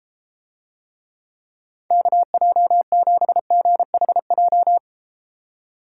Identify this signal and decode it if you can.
Morse code, heard as a keyed tone, about 32 wpm, 700 Hz. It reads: KJ7GHJ